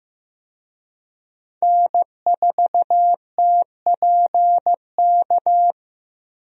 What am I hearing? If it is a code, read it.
Morse code, heard as a keyed tone, about 15 wpm, 700 Hz. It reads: N4TPK
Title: USA Call Signs - 5 Characters - 15wpm